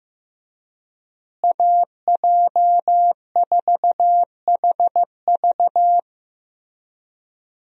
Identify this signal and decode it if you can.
Morse code, heard as a keyed tone, about 15 wpm, 700 Hz. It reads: AJ4HV
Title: USA Call Signs - 5 Characters - Repeated 3x - 15wpm